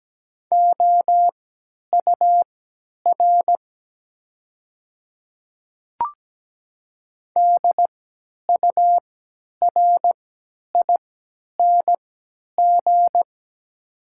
Morse code, heard as a keyed tone, about 17 wpm, spaced as for 10 wpm, 700 Hz. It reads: OUR DURING
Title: 3x Character Spacing - Top 500 Words - 17wpm